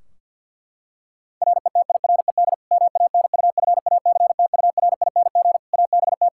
Morse code, heard as a keyed tone, about 52 wpm, 700 Hz. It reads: RETIRED MANUFACTURING ABT
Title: Practice QSO - Exchange 3 - More Info - 50wpm